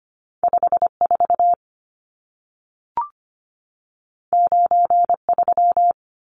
Morse code, heard as a keyed tone, about 25 wpm, 700 Hz. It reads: 5 4 9 3
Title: Numbers - 2-Digit 25wpm